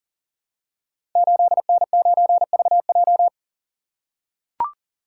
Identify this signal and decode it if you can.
Morse code, heard as a keyed tone, about 42 wpm, 690 Hz. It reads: 8N9VJ
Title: Code Groups - 5 Letter:Numbers - 40wpm